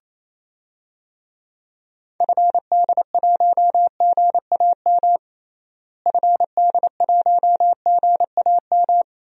Morse code, heard as a keyed tone, about 28 wpm, 700 Hz. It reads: FD1GAM FD1GAM
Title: DXCC-100 Call Signs - 6 Characters - Repeated 2x - 28wpm